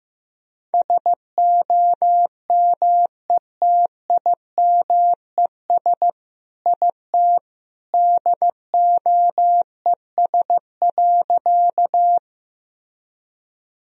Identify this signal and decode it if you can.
Morse code, heard as a keyed tone, about 16 wpm, 700 Hz. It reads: SOMETIMES IT DOES.
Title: Sentences from Top 1000 Words - Review Part 3 - 15wpm